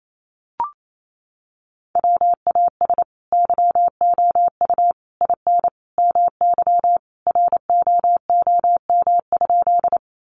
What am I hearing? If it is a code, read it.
Morse code, heard as a keyed tone, about 28 wpm, 700 Hz. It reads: WAS YOU IN MY ROOM?
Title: Sentences from Top 300 Words 28wpm